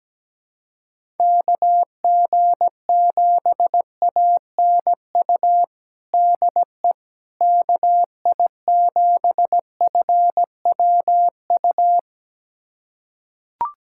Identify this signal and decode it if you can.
Morse code, heard as a keyed tone, about 17 wpm, 700 Hz. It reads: KG7ANU DE KI7FWU
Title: USA Callsign DE Callsign - 6 Characters - 17wpm